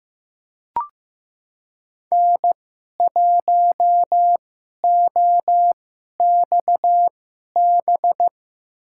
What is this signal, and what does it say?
Morse code, heard as a keyed tone, about 15 wpm, spaced as for 11 wpm, 700 Hz. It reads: N1OXB
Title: USA Call Signs - 5 Characters - 2x Character Spacing - 15wpm